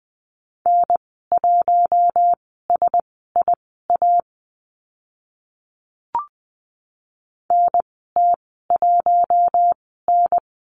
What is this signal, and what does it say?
Morse code, heard as a keyed tone, about 20 wpm, spaced as for 14 wpm, 700 Hz. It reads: N1SIA NT1N
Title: 042 Lesson 8g - Number 1 - Call Signs 20wpm